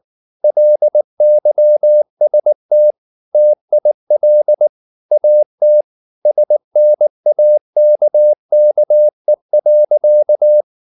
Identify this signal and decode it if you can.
Morse code, heard as a keyed tone, about 19 wpm, 600 Hz. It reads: LYST TIL AT SNAKKE.
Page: Ferie 19 wpm